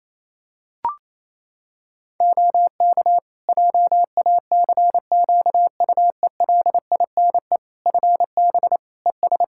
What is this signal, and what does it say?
Morse code, heard as a keyed tone, about 27 wpm, 700 Hz. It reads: OK JACQUELINE FB ES
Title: K1USN SST QSO Practice 28wpm